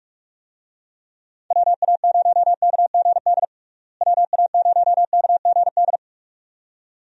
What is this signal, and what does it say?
Morse code, heard as a keyed tone, about 42 wpm, 700 Hz. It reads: WA0KGD WA0KGD
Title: USA Call Signs - 6 Characters - Repeated 2x - 45wpm